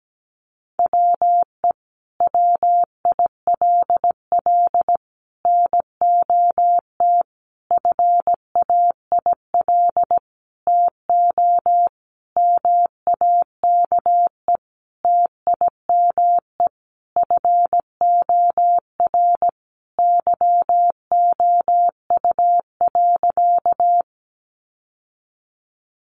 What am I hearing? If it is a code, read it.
Morse code, heard as a keyed tone, about 17 wpm, 700 Hz. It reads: WE WILL NOT FAIL TO MAKE TIME FOR YOU.
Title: Sentences from Top 1100 Words 17wpm